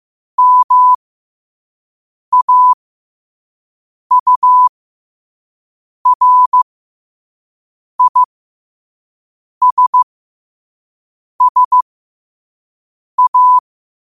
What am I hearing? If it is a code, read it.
Morse code, heard as a keyed tone, about 15 wpm, spaced as for 5 wpm, 1000 Hz. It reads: MAURISSA